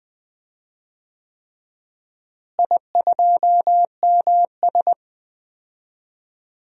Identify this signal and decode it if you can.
Morse code, heard as a keyed tone, about 20 wpm, 700 Hz. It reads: I2MS
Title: DXCC-100 Call Signs - 4 Characters - Repeated 3x - 20wpm